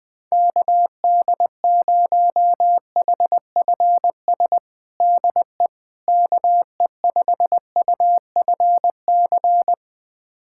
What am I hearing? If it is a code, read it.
Morse code, heard as a keyed tone, about 20 wpm, 700 Hz. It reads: KD0HFS DE KE5UFC
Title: USA Callsign DE Callsign - 6 Characters - 20wpm